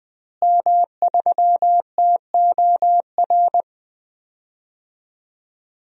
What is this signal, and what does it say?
Morse code, heard as a keyed tone, about 20 wpm, 700 Hz. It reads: M3TOR
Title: DXCC-100 Call Signs - 5 Characters - Repeated 3x - 20wpm